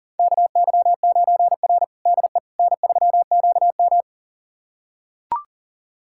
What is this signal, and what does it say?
Morse code, heard as a keyed tone, about 40 wpm, 700 Hz. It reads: KY9R DE N3QM
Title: USA Callsign DE Callsign - 4 Characters - 40wpm